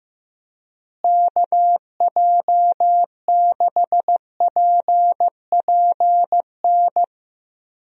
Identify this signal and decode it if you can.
Morse code, heard as a keyed tone, about 15 wpm, 700 Hz. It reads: KJ6PPN
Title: USA Call Signs - 6 Characters - Repeated 2x - 15wpm